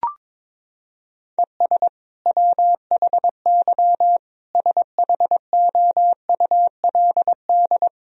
Morse code, heard as a keyed tone, about 22 wpm, 700 Hz. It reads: ES WHY SHOULD
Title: Sentences from Top 300 Words and Abbreviations 22wpm